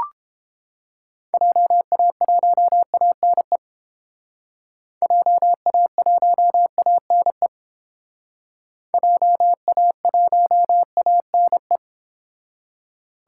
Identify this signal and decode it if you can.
Morse code, heard as a keyed tone, about 30 wpm, 700 Hz. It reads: JA1ANE JA1ANE JA1ANE